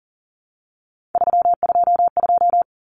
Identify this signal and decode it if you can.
Morse code, heard as a keyed tone, about 40 wpm, 700 Hz. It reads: 322
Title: Numbers - 3-Digit 40wpm